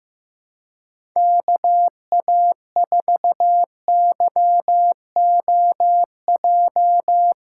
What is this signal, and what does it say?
Morse code, heard as a keyed tone, about 15 wpm, 700 Hz. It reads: KA4YOJ
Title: USA Callsign DE Callsign 15wpm